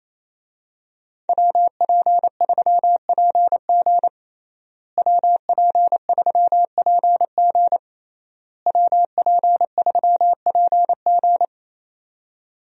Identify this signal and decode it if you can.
Morse code, heard as a keyed tone, about 27 wpm, 700 Hz. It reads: WP3PG WP3PG WP3PG
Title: USA Call Signs - 5 Characters - Repeated 3x - 28wpm